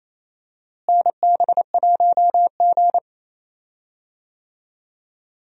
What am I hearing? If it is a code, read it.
Morse code, heard as a keyed tone, about 28 wpm, 700 Hz. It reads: NB1G
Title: USA Call Signs - 4 Characters - Repeated 3x - 28wpm